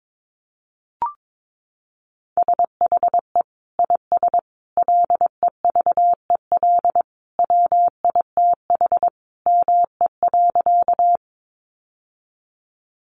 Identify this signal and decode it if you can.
Morse code, heard as a keyed tone, about 22 wpm, 700 Hz. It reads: SHE IS LEVEL WITH ME.
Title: Sentences from Top 700 Words 22wpm